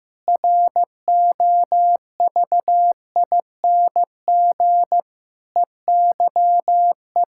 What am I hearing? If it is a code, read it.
Morse code, heard as a keyed tone, about 15 wpm, 700 Hz. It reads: ROVING EYE
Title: Pangram Challenge v2 - 15wpm